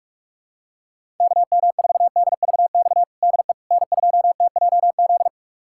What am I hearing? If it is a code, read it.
Morse code, heard as a keyed tone, about 46 wpm, 700 Hz. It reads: KM4DVX DE N2TJZ
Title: USA Callsign DE Callsign 45wpm